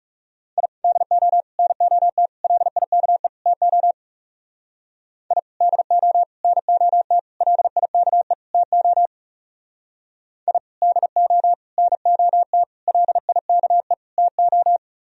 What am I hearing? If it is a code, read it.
Morse code, heard as a keyed tone, about 37 wpm, 700 Hz. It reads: I DO NOT LIKE TO I DO NOT LIKE TO I DO NOT LIKE TO
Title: Speed-Racing - Sets of 5 Words 30wpm